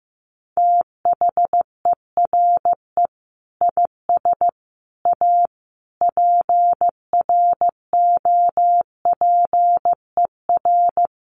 Morse code, heard as a keyed tone, about 15 wpm, 700 Hz. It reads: THERE IS A PROPER
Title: Sentences from Top 800 Words 15wpm